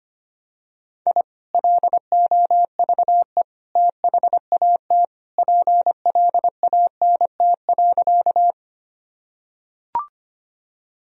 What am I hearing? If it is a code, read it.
Morse code, heard as a keyed tone, about 25 wpm, 700 Hz. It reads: I LOVE THAT PLANT.
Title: Sentences from Top 1000 Words - Review Part 1 - 25wpm